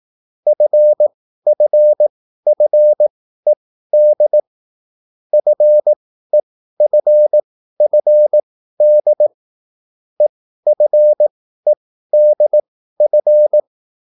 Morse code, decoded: FFFED FEFFD EFEDF

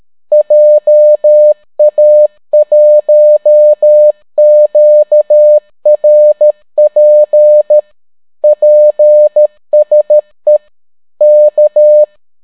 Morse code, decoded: JA1QRP PSE K

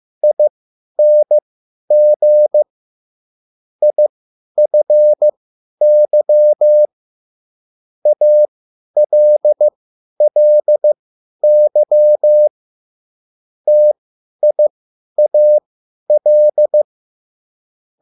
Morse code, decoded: ING IFY ALLY TIAL

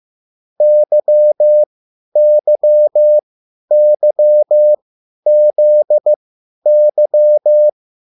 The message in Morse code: YYYZY

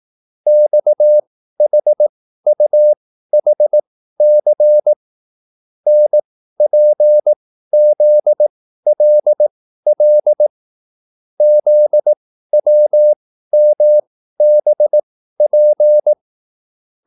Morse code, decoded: XHUHC NPZLL ZWMBP

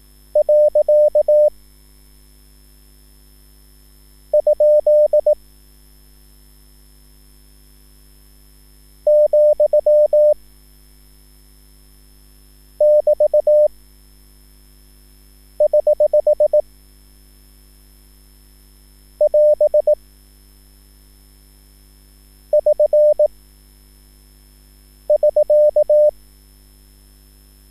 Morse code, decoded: . ? , <BT> <HH> <AS> <SN> <SK>

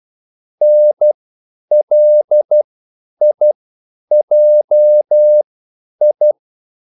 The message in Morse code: NLIJI

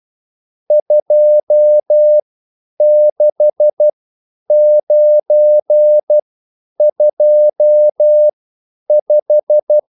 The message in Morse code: 26925

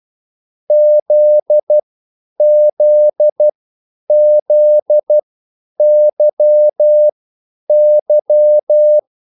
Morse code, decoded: ZZZYY

